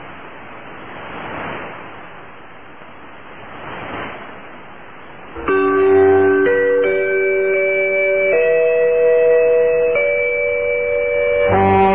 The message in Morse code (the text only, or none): none